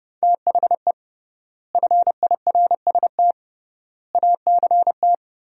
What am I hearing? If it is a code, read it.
Morse code, heard as a keyed tone, about 30 wpm, 700 Hz. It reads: THE FIRST ACT
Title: 3x Word-Spacing - Sentences from Top 200 Words - 30wpm